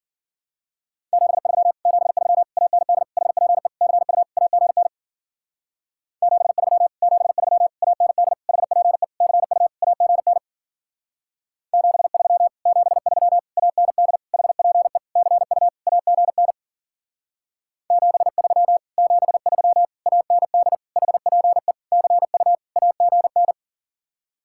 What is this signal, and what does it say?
Morse code, heard as a keyed tone, about 54 wpm, 700 Hz. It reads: 73 73 AND HPE CU AGN 73 73 AND HPE CU AGN 73 73 AND HPE CU AGN 73 73 AND HPE CU AGN